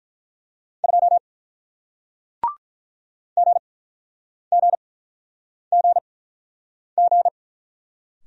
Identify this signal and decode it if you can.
Morse code, heard as a keyed tone, about 44 wpm, spaced as for 9 wpm, 700 Hz. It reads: 2 GGGG